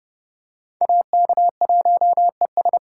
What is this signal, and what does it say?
Morse code, heard as a keyed tone, about 31 wpm, 700 Hz. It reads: AK1ES